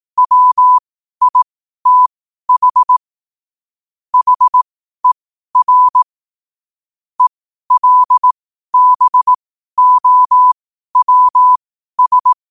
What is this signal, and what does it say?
Morse code, heard as a keyed tone, about 18 wpm, spaced as for 12 wpm, 1000 Hz. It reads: WITH HER ELBOWS
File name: qotdfiles / 12wpm